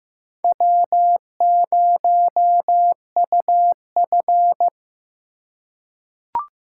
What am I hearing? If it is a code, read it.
Morse code, heard as a keyed tone, about 15 wpm, 700 Hz. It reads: W0UF
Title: K1USN SST QSO Practice - 15wpm